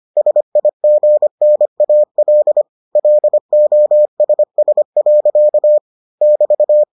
Morse code, heard as a keyed tone, about 25 wpm, 600 Hz. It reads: SIGNAL LOSS. <BT>